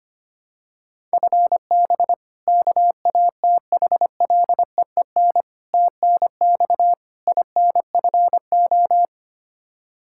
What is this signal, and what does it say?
Morse code, heard as a keyed tone, about 25 wpm, 700 Hz. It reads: FB KATHLEEN TNX INFO
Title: Practice QSO - Exchange 3 - More Info - 25wpm